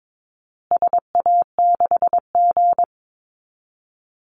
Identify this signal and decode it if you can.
Morse code, heard as a keyed tone, about 22 wpm, 700 Hz. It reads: SA6G
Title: Call Signs Repeated 2x - 22wpm